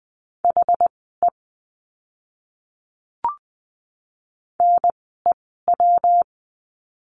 Morse code, heard as a keyed tone, about 20 wpm, spaced as for 15 wpm, 700 Hz. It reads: HE NEW
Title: 237 Lesson 38f - Letter q - Words 20wpm